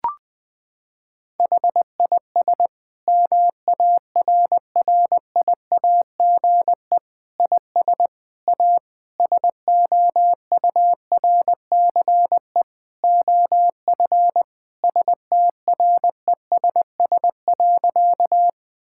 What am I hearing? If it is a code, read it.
Morse code, heard as a keyed tone, about 21 wpm, 700 Hz. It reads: HIS MARRIAGE IS A SOURCE OF STRESS.
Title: Sentences from Top 1000 Words - Review Part 6 - 20wpm